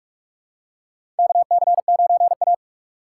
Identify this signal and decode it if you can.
Morse code, heard as a keyed tone, about 46 wpm, 700 Hz. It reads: KC9A
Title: USA Call Signs - 6x Character Spacing - 45wpm